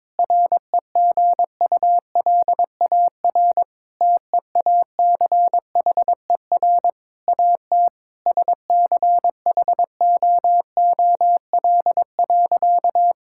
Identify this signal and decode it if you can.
Morse code, heard as a keyed tone, about 22 wpm, 700 Hz. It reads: REGULAR TEACHER AT SCHOOL.
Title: Sentences from Top 1000 Words - Review Part 6 - 22wpm